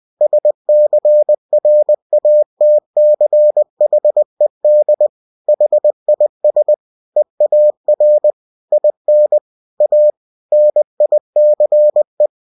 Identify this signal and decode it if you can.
Morse code, heard as a keyed tone, about 20 wpm, 600 Hz. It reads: SCRATCHED HIS EAR IN A NICE
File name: cw-books / winnie-the-pooh / winnie-20wpm-page-0022